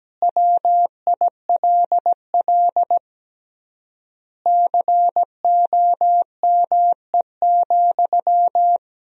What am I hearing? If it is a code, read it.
Morse code, heard as a keyed tone, about 17 wpm, 700 Hz. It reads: WILL COME,
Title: 3x Word-Spacing - Sentences from Top 100 Words - 17wpm